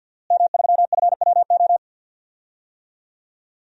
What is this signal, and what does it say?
Morse code, heard as a keyed tone, about 51 wpm, 700 Hz. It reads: M3FWO
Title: DXCC-100 Call Signs - 5 Characters - Repeated 3x - 50wpm